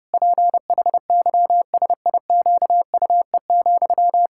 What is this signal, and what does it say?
Morse code, heard as a keyed tone, about 30 wpm, 700 Hz. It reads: PHYSIQUE,
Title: Pangram Challenge v2 - 30wpm